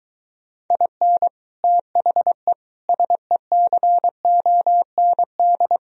Morse code, heard as a keyed tone, about 24 wpm, 700 Hz. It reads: IN THE SECOND